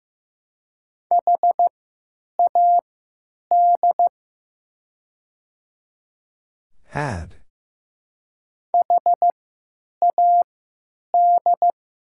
Morse code, heard as a keyed tone, about 15 wpm, spaced as for 8 wpm, 700 Hz. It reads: HAD HAD
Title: 3x Character Spacing - Top 200 Words - 15wpm